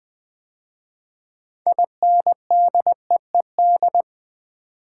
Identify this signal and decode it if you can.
Morse code, heard as a keyed tone, about 20 wpm, 700 Hz. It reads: INDEED
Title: 6 Letter Words - First 3 Letters Given - 20wpm